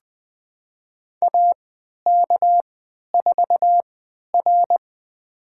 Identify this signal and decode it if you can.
Morse code, heard as a keyed tone, about 21 wpm, spaced as for 11 wpm, 700 Hz. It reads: AK4R
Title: USA Call Signs - 4 Characters - 3x Character Spacing - 20wpm